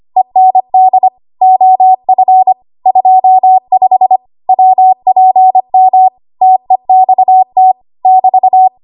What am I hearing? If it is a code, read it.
Morse code, heard as a keyed tone, about 25 wpm, 750 Hz. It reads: END OF 25 WPM TEXT <BT>
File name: Archive / 25 WPM / 25WPM